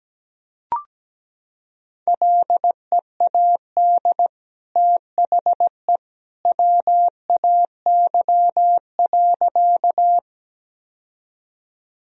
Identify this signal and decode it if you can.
Morse code, heard as a keyed tone, about 17 wpm, 700 Hz. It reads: LEAD THE WAY.
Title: Sentences from Top 1000 Words - Review Part 2 - 17wpm